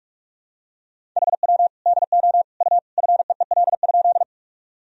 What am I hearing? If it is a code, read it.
Morse code, heard as a keyed tone, about 45 wpm, 700 Hz. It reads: HW DO U FEEL?
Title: Sentences from Top 400 Words and Abbreviations 45wpm